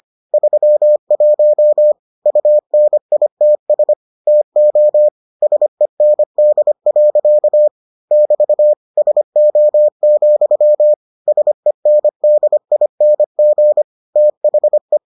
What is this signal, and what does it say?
Morse code, heard as a keyed tone, about 25 wpm, 600 Hz. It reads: 31 UNITS TO SEND. <BT> SO, SENDING THE